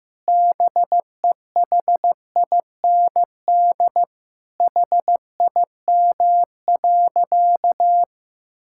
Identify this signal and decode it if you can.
Morse code, heard as a keyed tone, about 15 wpm, 700 Hz. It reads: BEHIND HIM.
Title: Sentences from Top 1600 Words 15wpm